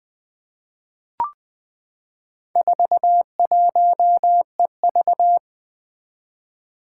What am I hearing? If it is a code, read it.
Morse code, heard as a keyed tone, about 20 wpm, 700 Hz. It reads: 41EV
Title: Four Challenging Letter-Number Combinations 20wpm